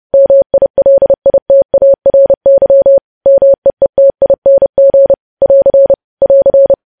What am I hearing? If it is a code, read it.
Morse code, heard as a keyed tone, about 30 wpm, 550 Hz. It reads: MILITARY MEETING <AR> <AR>
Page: news headlines in morse code at 30 wpm